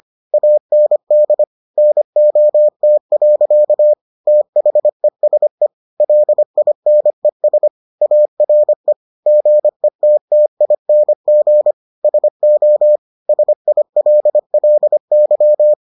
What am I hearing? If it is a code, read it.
Morse code, heard as a keyed tone, about 25 wpm, 600 Hz. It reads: AND NOT. THESE LINES ARE GETTING SO SILLY